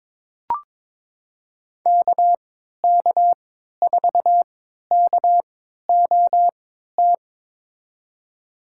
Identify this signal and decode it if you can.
Morse code, heard as a keyed tone, about 21 wpm, spaced as for 12 wpm, 700 Hz. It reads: KK4KOT
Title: USA Call Signs - 6 Characters - 3x Character Spacing - 22wpm